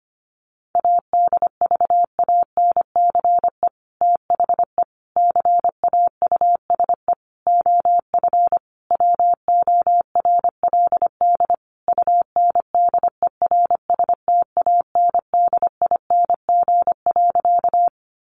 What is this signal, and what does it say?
Morse code, heard as a keyed tone, about 25 wpm, 700 Hz. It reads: ADVANCE THE CAUSE OF WORLD UNDERSTANDING.
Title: Sentences from Top 1400 Words 25wpm